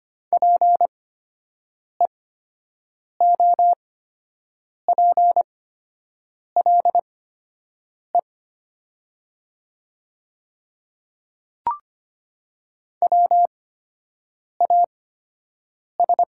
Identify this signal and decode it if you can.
Morse code, heard as a keyed tone, about 25 wpm, spaced as for 7 wpm, 700 Hz. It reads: PEOPLE WAS
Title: 8x Character Spacing - Top 100 Words - 25wpm